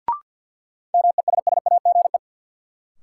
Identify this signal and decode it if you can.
Morse code, heard as a keyed tone, about 52 wpm, 700 Hz. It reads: MESSAGE
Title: Sentences from Top 1100 Words - New Word Drill Alt - 50wpm